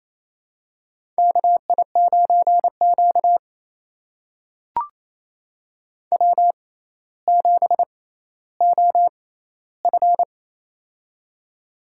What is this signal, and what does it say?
Morse code, heard as a keyed tone, about 28 wpm, spaced as for 14 wpm, 700 Hz. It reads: KI9Q W7OF
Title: USA Call Signs - 4 Characters - 6x Character Spacing - 28wpm